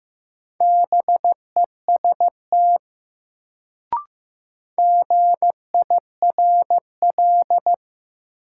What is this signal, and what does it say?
Morse code, heard as a keyed tone, about 15 wpm, 700 Hz. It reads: BEST GIRL
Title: Sentences from Top 400 Words - New Word Drill - 15wpm